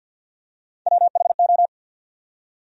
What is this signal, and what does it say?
Morse code, heard as a keyed tone, about 48 wpm, 700 Hz. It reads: WHO
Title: Top 100 Words 50wpm